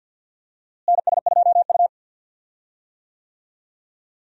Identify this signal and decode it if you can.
Morse code, heard as a keyed tone, about 52 wpm, 700 Hz. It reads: NS2U
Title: USA Call Signs - 4 Characters - Repeated 3x - 50wpm